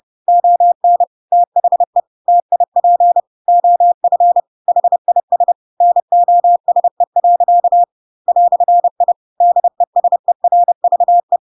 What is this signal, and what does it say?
Morse code, heard as a keyed tone, about 30 wpm, 700 Hz. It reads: ON THE TIP OF HIS NOSE. "I DESERVE
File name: Pinocchio / Pinocchio ch3 30 30 ik1hge